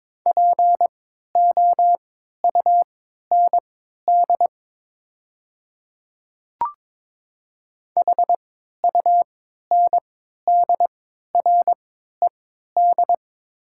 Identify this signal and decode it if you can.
Morse code, heard as a keyed tone, about 22 wpm, spaced as for 13 wpm, 700 Hz. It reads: POUND HUNDRED
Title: 3x Character Spacing - Top 500 Words - 22wpm